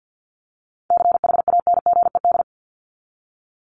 Morse code, heard as a keyed tone, about 50 wpm, 700 Hz. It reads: CHANGED
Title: ED Suffix - Top 100 Words with Base Words 50wpm